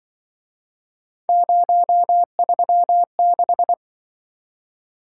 Morse code, heard as a keyed tone, about 24 wpm, 700 Hz. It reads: 036
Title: Speed-Racing - Three Letter-Number Combinations - 20wpm